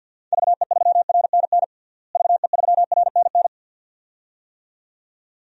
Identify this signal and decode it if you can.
Morse code, heard as a keyed tone, about 50 wpm, 700 Hz. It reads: VE3RNN VE3RNN
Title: DXCC-100 Call Signs - 6 Characters - Repeated 2x - 50wpm